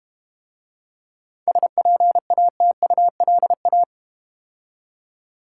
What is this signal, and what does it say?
Morse code, heard as a keyed tone, about 31 wpm, 700 Hz. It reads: SPATULA
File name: Easy Words - Speed-Racing with Hints - 25wpm